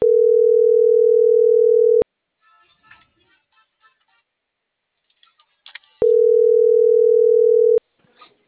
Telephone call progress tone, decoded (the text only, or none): ringback tone